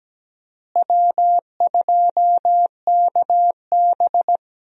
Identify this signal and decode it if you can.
Morse code, heard as a keyed tone, about 17 wpm, 700 Hz. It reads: W2KB
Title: USA Call Signs - 4 Characters - Repeated 2x - 17wpm